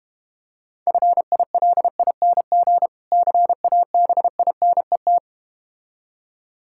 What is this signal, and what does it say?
Morse code, heard as a keyed tone, about 32 wpm, 700 Hz. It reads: FILING CABINET